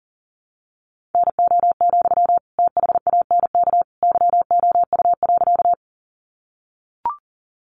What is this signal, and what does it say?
Morse code, heard as a keyed tone, about 40 wpm, 700 Hz. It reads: NO, THANK YOU.
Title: Sentences from Top 600 Words 40wpm